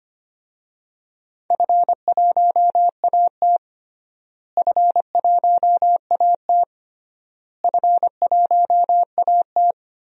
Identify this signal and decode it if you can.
Morse code, heard as a keyed tone, about 25 wpm, 700 Hz. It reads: F1AT F1AT F1AT
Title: DXCC-100 Call Signs - 4 Characters - Repeated 3x - 25wpm